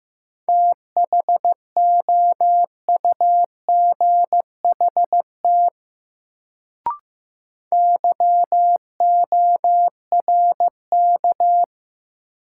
Morse code, heard as a keyed tone, about 15 wpm, 700 Hz. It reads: THOUGHT YORK
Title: Sets of 2 Words - New Word Drill - 15wpm